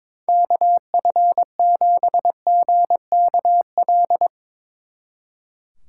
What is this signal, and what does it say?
Morse code, heard as a keyed tone, about 30 wpm, spaced as for 27 wpm, 700 Hz. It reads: KF7GKL
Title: Speed Racing - USA 2x3 Call Signs 22wpm